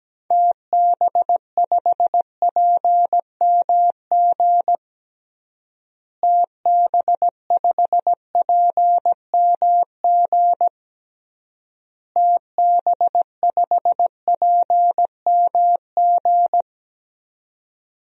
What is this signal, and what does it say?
Morse code, heard as a keyed tone, about 17 wpm, 700 Hz. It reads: TB5PMG TB5PMG TB5PMG
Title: DXCC-100 Call Signs - 6 Characters - Repeated 3x - 17wpm